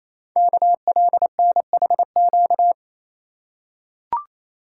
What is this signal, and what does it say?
Morse code, heard as a keyed tone, about 27 wpm, 700 Hz. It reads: KLNHQ
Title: Code Groups - 5 Letters - 28wpm